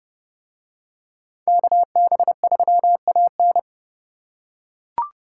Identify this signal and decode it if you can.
Morse code, heard as a keyed tone, about 29 wpm, 700 Hz. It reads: KB3AN